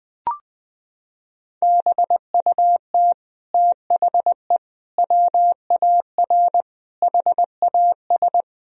Morse code, heard as a keyed tone, about 20 wpm, 700 Hz. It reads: BUT THE WAR HAS
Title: Sentences from Top 1000 Words - Review Part 2 - 20wpm